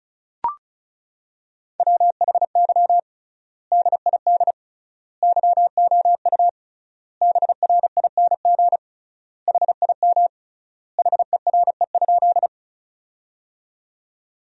Morse code, heard as a keyed tone, about 36 wpm, 700 Hz. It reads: WHY DID YOU BRING HIM HERE?
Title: 3x Word-Spacing - Sentences from Top 500 Words - 35wpm